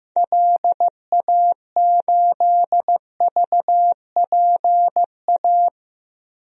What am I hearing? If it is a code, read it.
Morse code, heard as a keyed tone, about 15 wpm, 700 Hz. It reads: LA8VPA
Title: DXCC-100 Call Signs - 6 Characters - 15wpm